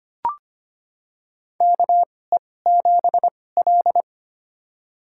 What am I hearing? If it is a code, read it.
Morse code, heard as a keyed tone, about 25 wpm, spaced as for 18 wpm, 700 Hz. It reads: KE7L
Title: USA Call Signs - 4 Characters - 2x Character Spacing - 25wpm